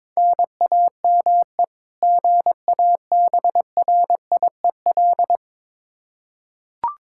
Morse code, heard as a keyed tone, about 22 wpm, 700 Hz. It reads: NAME GABRIEL
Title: K1USN SST QSO Practice 22wpm